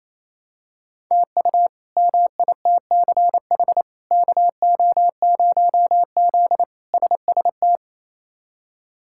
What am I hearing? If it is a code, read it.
Morse code, heard as a keyed tone, about 28 wpm, 700 Hz. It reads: TU MITCH KO0Z SST
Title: K1USN SST QSO Practice - 28wpm